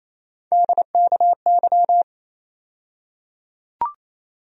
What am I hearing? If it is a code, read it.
Morse code, heard as a keyed tone, about 28 wpm, 700 Hz. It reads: DKY
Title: USA Call Sign Postfixes - 3 Letters 28wpm